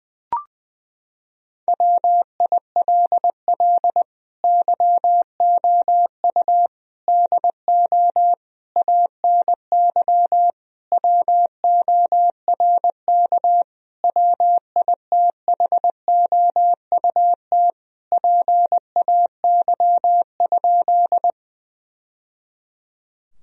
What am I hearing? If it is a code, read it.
Morse code, heard as a keyed tone, about 20 wpm, 700 Hz. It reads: WILL YOU DO ANY WORK WITHOUT PAY?
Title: Sentences from Top 1000 Words - Review Part 3 - 20wpm